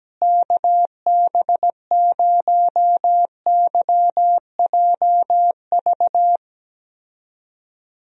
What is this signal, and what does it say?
Morse code, heard as a keyed tone, about 17 wpm, 700 Hz. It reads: KB0YJV
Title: USA Call Signs - 6 Characters - Repeated 2x - 17wpm